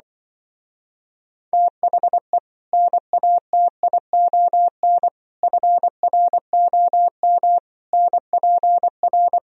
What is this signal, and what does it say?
Morse code, heard as a keyed tone, about 24 wpm, 700 Hz. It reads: THE NATION FROM NPR
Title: Speed-Racing - Sets of 5 Words 20wpm